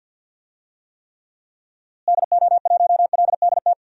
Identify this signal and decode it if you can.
Morse code, heard as a keyed tone, about 46 wpm, 700 Hz. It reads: DO1LDT
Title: DXCC-100 Call Signs - 6 Characters - Repeated 2x - 50wpm